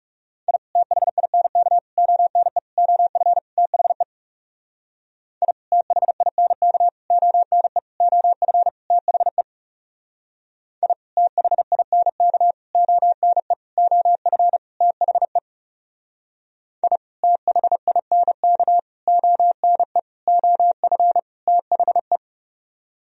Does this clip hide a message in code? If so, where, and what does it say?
Yes: Morse code from 0.5 to 22.2 s, heard as a keyed tone, about 38 wpm, 700 Hz. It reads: I THINK ONE OF THE I THINK ONE OF THE I THINK ONE OF THE I THINK ONE OF THE